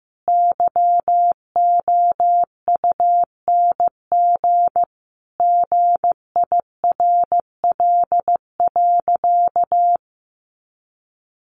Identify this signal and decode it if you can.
Morse code, heard as a keyed tone, about 14 wpm, 700 Hz. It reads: YOUNG GIRL.
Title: Sentences from Top 1400 Words 15wpm